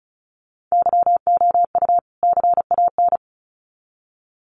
Morse code, heard as a keyed tone, about 34 wpm, 700 Hz. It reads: YOU CAN
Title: Sets of 2 Words 35wpm